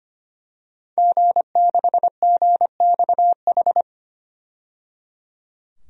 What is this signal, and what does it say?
Morse code, heard as a keyed tone, about 25 wpm, 700 Hz. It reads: G6GXH
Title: DXCC-100 Call Signs - 5 Characters - 25wpm